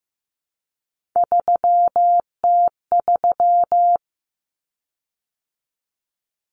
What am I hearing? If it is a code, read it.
Morse code, heard as a keyed tone, about 15 wpm, 700 Hz. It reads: 3T3
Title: Common Cut Numbers - 3-Digit 15wpm